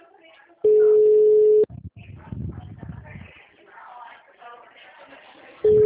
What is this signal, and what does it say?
Ringback tone (Europe, 425 Hz), with speech.